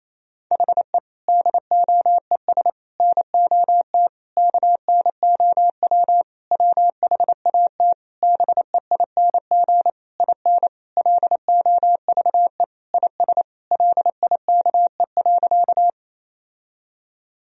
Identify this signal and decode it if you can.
Morse code, heard as a keyed tone, about 28 wpm, 700 Hz. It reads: HE DOES NOT KNOW WHAT BEING IN LOVE IS LIKE.
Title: Sentences from Top 1000 Words - Review Part 3 - 28wpm